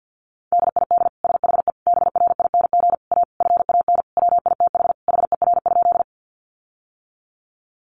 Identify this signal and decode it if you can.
Morse code, heard as a keyed tone, about 50 wpm, 700 Hz. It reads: DID SHE BRING A FAN WITH HER?
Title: Sentences from Top 1300 Words 50wpm